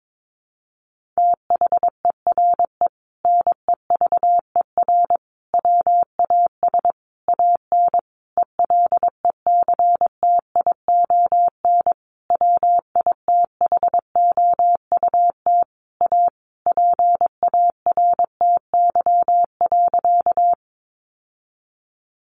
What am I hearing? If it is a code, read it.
Morse code, heard as a keyed tone, about 22 wpm, 700 Hz. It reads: THERE NEVER WAS AN ELECTION WITHOUT A PARTY.
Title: Sentences from Top 1500 Words 22wpm